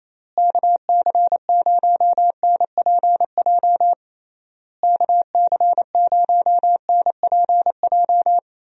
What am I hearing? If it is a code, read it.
Morse code, heard as a keyed tone, about 28 wpm, 700 Hz. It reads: KC0NPJ KC0NPJ
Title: USA Call Signs - 6 Characters - Repeated 3x - 28wpm